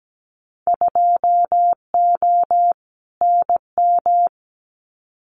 Morse code, heard as a keyed tone, about 17 wpm, 700 Hz. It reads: 2O NM
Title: Winter Field Day Exchanges 17wpm